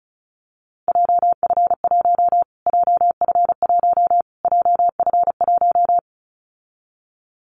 Morse code, heard as a keyed tone, about 34 wpm, 700 Hz. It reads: JF1 JF1 JF1
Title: Callsign Prefixes for Top 25 DXCC - Repeated 3x - 35wpm